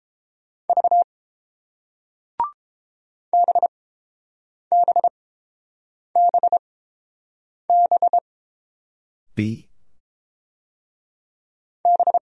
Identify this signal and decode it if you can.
Morse code, heard as a keyed tone, about 30 wpm, spaced as for 7 wpm, 700 Hz. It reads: V BBBB B